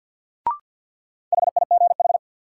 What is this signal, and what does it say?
Morse code, heard as a keyed tone, about 52 wpm, 700 Hz. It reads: HIGH
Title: Sets of 2 Words - New Word Drill - 50wpm